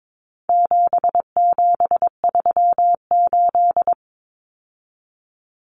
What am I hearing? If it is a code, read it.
Morse code, heard as a keyed tone, about 22 wpm, 700 Hz. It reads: 7738
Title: Numbers - 4-Digit 22wpm